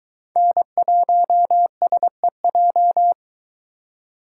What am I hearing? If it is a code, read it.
Morse code, heard as a keyed tone, about 22 wpm, 700 Hz. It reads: N1SEJ